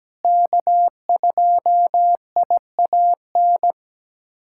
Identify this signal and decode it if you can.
Morse code, heard as a keyed tone, about 17 wpm, 700 Hz. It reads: K2IAN
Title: USA Callsign DE Callsign - 5 Characters - 17wpm